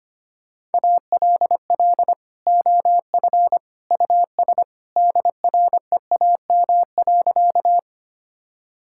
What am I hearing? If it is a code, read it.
Morse code, heard as a keyed tone, about 25 wpm, 700 Hz. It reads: ALL OF US DREAM.
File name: practice / 25wpm / Sentences with 2-Word Phrases - 25wpm